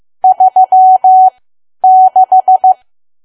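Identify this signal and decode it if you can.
Morse code, heard as a keyed tone, about 15 wpm, spaced as for 10 wpm, 750 Hz. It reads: 36
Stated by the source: Morse learning practice at 10 wpm.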